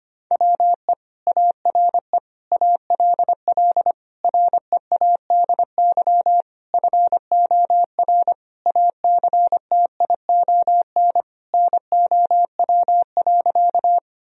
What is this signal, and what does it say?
Morse code, heard as a keyed tone, about 25 wpm, 700 Hz. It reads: WE ARE ALL READY FOR ACTION NOW.
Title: Sentences from Top 1000 Words - Review Part 4 - 25wpm